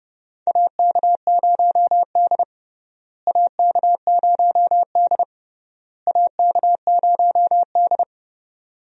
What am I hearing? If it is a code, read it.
Morse code, heard as a keyed tone, about 29 wpm, 700 Hz. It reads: AK0D AK0D AK0D
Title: USA Call Signs - 4 Characters - Repeated 3x - 30wpm